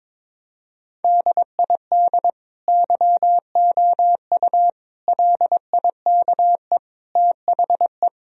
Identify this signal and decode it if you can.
Morse code, heard as a keyed tone, about 22 wpm, 700 Hz. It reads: DID YOU LIKE THE
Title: Sentences from Top 1000 Words - Review Part 4 - 22wpm